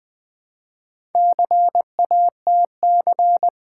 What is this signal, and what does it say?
Morse code, heard as a keyed tone, about 20 wpm, 700 Hz. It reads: CATC
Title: Build A Word - 1 Letter At A Time - 5-Letter Words - 20wpm